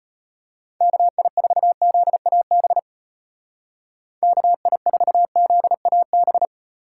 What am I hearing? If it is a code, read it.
Morse code, heard as a keyed tone, about 32 wpm, 700 Hz. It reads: KI4ZAB KI4ZAB